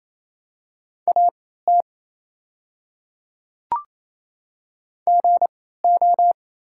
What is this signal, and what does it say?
Morse code, heard as a keyed tone, about 28 wpm, spaced as for 16 wpm, 700 Hz. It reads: AT GO
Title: 3x Character Spacing - Top 100 Words - 28wpm